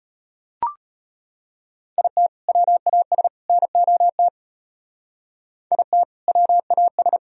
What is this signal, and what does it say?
Morse code, heard as a keyed tone, about 30 wpm, 700 Hz. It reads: IT WAS NOT IT WAS
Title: Speed-Racing - Sets of 3 Words 25wpm